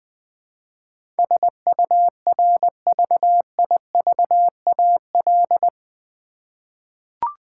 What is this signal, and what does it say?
Morse code, heard as a keyed tone, about 20 wpm, 700 Hz. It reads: SURVIVAL
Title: 8 Letter Words - First 3 Letters Given - 20wpm